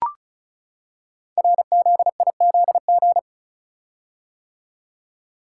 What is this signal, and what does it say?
Morse code, heard as a keyed tone, about 36 wpm, 700 Hz. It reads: RZIZG
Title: Code Groups - 5 Letters - 35wpm